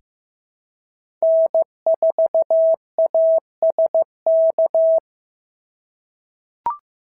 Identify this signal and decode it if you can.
Morse code, heard as a keyed tone, about 14 wpm, 650 Hz. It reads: N4ASK
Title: SOTA QSO Practice 15wpm